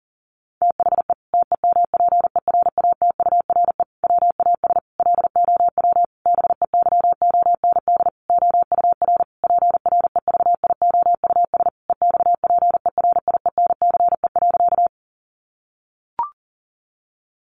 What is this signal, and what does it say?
Morse code, heard as a keyed tone, about 40 wpm, 700 Hz. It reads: THE TEMPERATURE WAS LOW BEYOND OUR PREVIOUS EXPERIENCE.
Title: Sentences from Top 1500 Words 40wpm